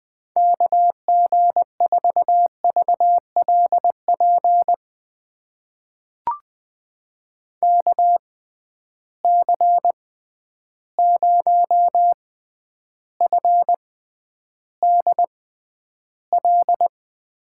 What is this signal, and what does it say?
Morse code, heard as a keyed tone, about 20 wpm, spaced as for 10 wpm, 700 Hz. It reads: KG4VLP KC0FDL
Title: USA Call Signs - 6 Characters - 6x Character Spacing - 20wpm